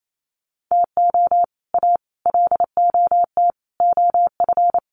Morse code, heard as a keyed tone, about 28 wpm, 700 Hz. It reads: TO A LOT OF
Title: Sets of 5 Words Encore 28wpm